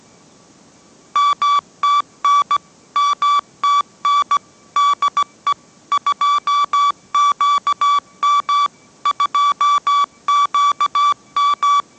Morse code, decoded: MTN MTN DE 2QM 2QM

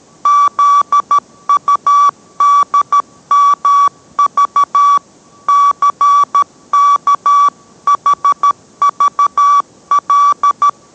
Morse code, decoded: ZUDMV CKHVL

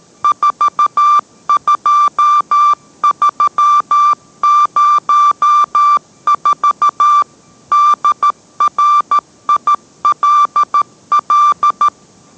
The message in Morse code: 42304 DRILL

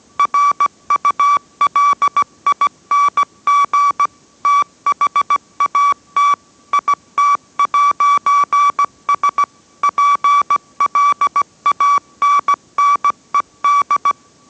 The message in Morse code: RULING THAT IT'S PLANNED

